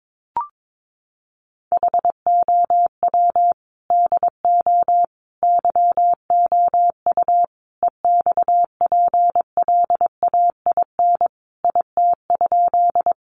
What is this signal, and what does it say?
Morse code, heard as a keyed tone, about 22 wpm, 700 Hz. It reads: HOW DO YOU EXPLAIN IT?
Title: Sentences from Top 1100 Words 22wpm